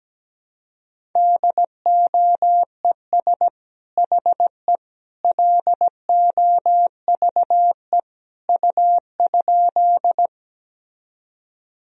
Morse code, heard as a keyed tone, about 17 wpm, 700 Hz. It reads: DOES HE LOVE U?
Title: Sentences from Top 400 Words and Abbreviations 17wpm